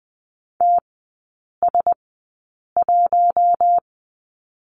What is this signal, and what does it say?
Morse code, heard as a keyed tone, about 20 wpm, spaced as for 8 wpm, 700 Hz. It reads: TS1
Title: 074 Lesson 13e - Letter L - Three Characters 20wpm